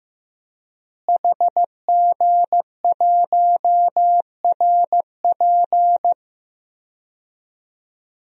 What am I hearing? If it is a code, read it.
Morse code, heard as a keyed tone, about 16 wpm, 700 Hz. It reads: HG1RP
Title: DXCC-100 Call Signs - 5 Characters - Repeated 3x - 15wpm